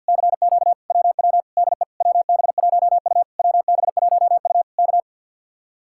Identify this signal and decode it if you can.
Morse code, heard as a keyed tone, about 62 wpm, 700 Hz. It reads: CQ WW DE WB1U WB1U K
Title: Speed-Racing Common QSO Phrases - 50wpm